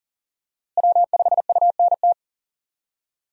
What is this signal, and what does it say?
Morse code, heard as a keyed tone, about 40 wpm, 700 Hz. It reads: W5UNT
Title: USA Call Signs - 5 Characters - Repeated 3x - 40wpm